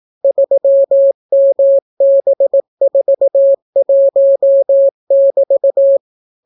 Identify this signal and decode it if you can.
Morse code, heard as a keyed tone, about 18 wpm, 550 Hz. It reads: 3MB41<BT>